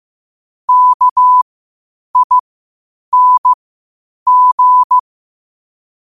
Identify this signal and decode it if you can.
Morse code, heard as a keyed tone, about 15 wpm, spaced as for 8 wpm, 1000 Hz. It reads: KING
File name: qotdfiles / 8wpm